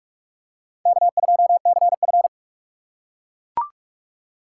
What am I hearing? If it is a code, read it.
Morse code, heard as a keyed tone, about 43 wpm, 700 Hz. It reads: K2CF
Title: USA Call Signs - 4 Characters - 6x Character Spacing - 45wpm